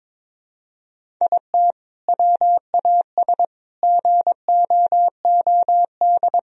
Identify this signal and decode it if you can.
Morse code, heard as a keyed tone, about 22 wpm, 700 Hz. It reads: IT WAS GOOD
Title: Sentences from Top 1000 Words - Review Part 1 - 22wpm